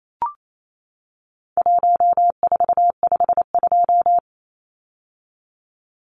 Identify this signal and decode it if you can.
Morse code, heard as a keyed tone, about 28 wpm, 700 Hz. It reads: 1452
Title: Numbers - Contest Style - 4-Digit 28wpm